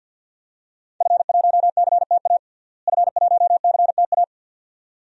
Morse code, heard as a keyed tone, about 51 wpm, 700 Hz. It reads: F1CTA F1CTA
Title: DXCC-100 Call Signs - 5 Characters - Repeated 2x - 50wpm